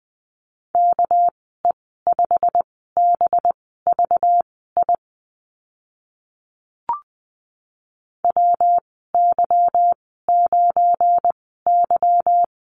Morse code, heard as a keyed tone, about 20 wpm, spaced as for 15 wpm, 700 Hz. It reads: KE5BVI WY9Y
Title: 200 Lesson 32g - Letter k - Call Signs 20wpm